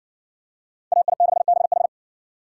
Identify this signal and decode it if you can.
Morse code, heard as a keyed tone, about 53 wpm, 700 Hz. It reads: AI6BH